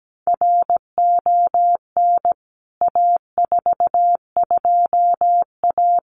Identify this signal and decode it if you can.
Morse code, heard as a keyed tone, about 17 wpm, 700 Hz. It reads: RON A42A
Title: CWT Exchange 17wpm